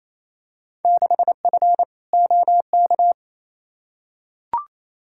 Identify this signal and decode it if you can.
Morse code, heard as a keyed tone, about 28 wpm, 700 Hz. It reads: 6F OK
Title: Field Day Exchanges Encore - 28wpm